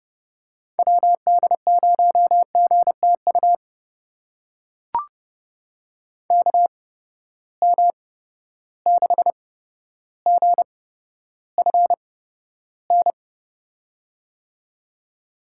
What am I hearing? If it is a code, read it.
Morse code, heard as a keyed tone, about 30 wpm, spaced as for 13 wpm, 700 Hz. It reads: WD0GTU KM6GFN
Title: USA Call Signs - 6 Characters - 8x Character Spacing - 30wpm